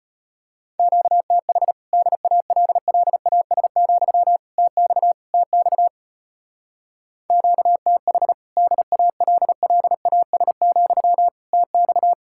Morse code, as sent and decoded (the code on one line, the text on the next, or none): --.- - .... / -.. .- .-.. .-.. .- ... --..-- / - -..- / - -..- / --.- - .... / -.. .- .-.. .-.. .- ... --..-- / - -..-
QTH DALLAS, TX TX QTH DALLAS, TX